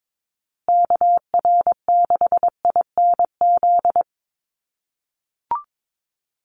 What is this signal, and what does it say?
Morse code, heard as a keyed tone, about 22 wpm, 700 Hz. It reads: KR6INZ
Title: USA Call Signs Repeated 2x 22wpm